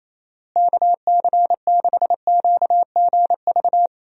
Morse code, heard as a keyed tone, about 28 wpm, 700 Hz. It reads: KC6QGV
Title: USA Call Signs - 6 Characters - Repeated 3x - 28wpm